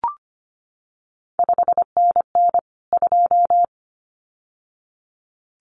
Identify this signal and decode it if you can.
Morse code, heard as a keyed tone, about 25 wpm, 700 Hz. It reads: 5NN 2
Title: CQ WW DX Contest Exchange Encore - 25wpm